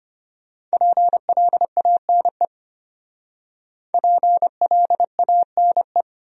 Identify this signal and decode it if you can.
Morse code, heard as a keyed tone, about 34 wpm, 700 Hz. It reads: PLANE PLANE